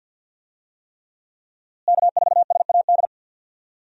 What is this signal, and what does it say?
Morse code, heard as a keyed tone, about 52 wpm, 700 Hz. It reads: K4SAD
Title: USA Call Signs - 5 Characters - Repeated 3x - 50wpm